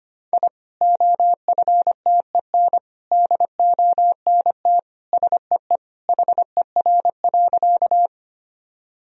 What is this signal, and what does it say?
Morse code, heard as a keyed tone, about 25 wpm, 700 Hz. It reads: I OFTEN DONT SEE HER.
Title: Sentences from Top 1000 Words - Review Part 1 - 25wpm